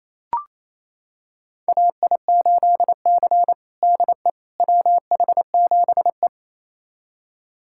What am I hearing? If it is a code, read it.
Morse code, heard as a keyed tone, about 28 wpm, 700 Hz. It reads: AI8C DE WH7E
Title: USA Callsign DE Callsign - 4 Characters - 28wpm